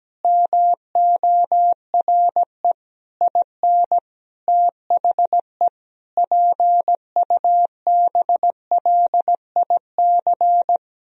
Morse code, heard as a keyed tone, about 17 wpm, 700 Hz. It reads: MORE IN THE PUBLIC
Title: Sentences from Top 1000 Words - Review Part 3 - 17wpm